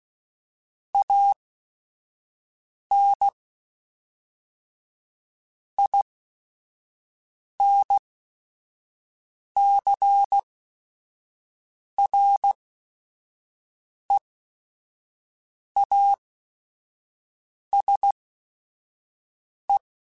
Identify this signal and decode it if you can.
Morse code, heard as a keyed tone, about 16 wpm, spaced as for 5 wpm, 800 Hz. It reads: AN INCREASE
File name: Technical Subjects / Technical Subjects-mp3s / XC @5